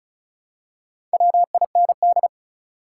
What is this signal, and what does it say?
Morse code, heard as a keyed tone, about 36 wpm, 700 Hz. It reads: WIND
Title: Four Letter Words - 35wpm